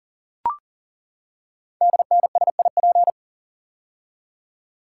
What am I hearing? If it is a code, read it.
Morse code, heard as a keyed tone, about 38 wpm, 700 Hz. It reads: DNSIP
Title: Code Groups - 5 Letters - 40wpm